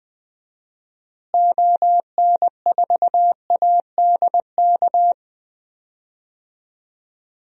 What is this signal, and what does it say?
Morse code, heard as a keyed tone, about 20 wpm, 700 Hz. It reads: ON4ADK
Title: DXCC-100 Call Signs - 6 Characters - Repeated 3x - 20wpm